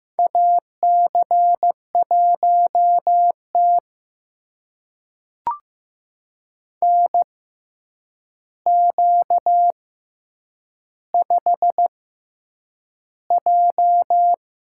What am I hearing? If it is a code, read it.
Morse code, heard as a keyed tone, about 15 wpm, spaced as for 8 wpm, 700 Hz. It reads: AC1T NQ5J
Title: USA Call Signs - 4 Characters - 6x Character Spacing - 15wpm